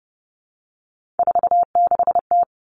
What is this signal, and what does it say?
Morse code, heard as a keyed tone, about 30 wpm, 700 Hz. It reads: 46T
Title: Three Challenging Letter-Number Combinations 30wpm